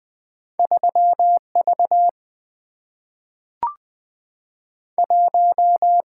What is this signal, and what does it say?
Morse code, heard as a keyed tone, about 20 wpm, 700 Hz. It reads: 3 V 1
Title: Single-Letter Single-Number 20wpm